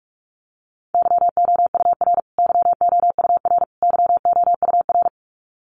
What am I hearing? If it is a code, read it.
Morse code, heard as a keyed tone, about 44 wpm, 700 Hz. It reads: YOUR YOUR YOUR
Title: Top 300 Words Repeated 45wpm